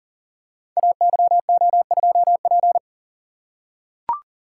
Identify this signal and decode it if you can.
Morse code, heard as a keyed tone, about 42 wpm, 690 Hz. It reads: AYO2P
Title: Code Groups - 5 Letter:Numbers - 40wpm